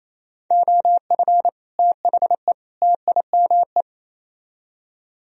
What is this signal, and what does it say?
Morse code, heard as a keyed tone, about 35 wpm, 700 Hz. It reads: OF THE TIME
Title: Speed-Racing - Sets of 3 Words 28wpm